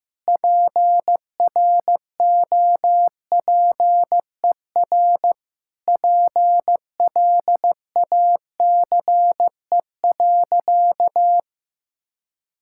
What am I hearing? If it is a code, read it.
Morse code, heard as a keyed tone, about 15 wpm, 700 Hz. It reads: PROPER PLACE.
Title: Sentences from Top 1000 Words - Review Part 5 - 15wpm